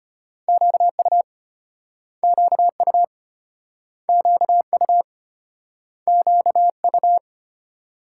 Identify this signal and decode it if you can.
Morse code, heard as a keyed tone, about 32 wpm, 700 Hz. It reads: QU QU QU QU